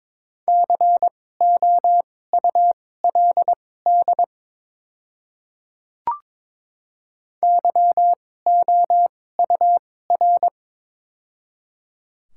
Morse code, heard as a keyed tone, about 22 wpm, spaced as for 16 wpm, 700 Hz. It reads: COULD YOUR
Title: 2x Character Spacing - Top 200 Words - 22wpm